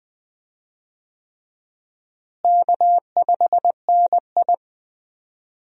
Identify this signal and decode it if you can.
Morse code, heard as a keyed tone, about 20 wpm, 700 Hz. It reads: K5NI
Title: USA Call Signs - 4 Characters - Repeated 3x - 20wpm